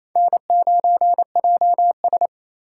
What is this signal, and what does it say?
Morse code, heard as a keyed tone, about 28 wpm, 700 Hz. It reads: N9JS
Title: USA Call Signs - 4 Characters - 3x Character Spacing - 28wpm